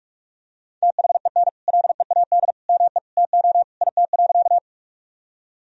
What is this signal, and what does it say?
Morse code, heard as a keyed tone, about 42 wpm, 700 Hz. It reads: THEN LEAD ME TO IT.
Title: Sentences from Top 1000 Words - Review Part 2 - 45wpm